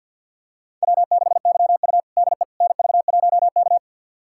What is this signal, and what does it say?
Morse code, heard as a keyed tone, about 52 wpm, 700 Hz. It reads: W6YU DE NV1K
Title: USA Callsign DE Callsign - 4 Characters - 50wpm